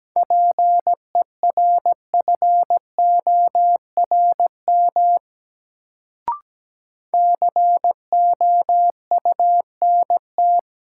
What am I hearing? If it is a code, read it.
Morse code, heard as a keyed tone, about 17 wpm, 700 Hz. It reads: PERFORM COUNT
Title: Sentences from Top 1500 Words - New Word Drill Alt - 17wpm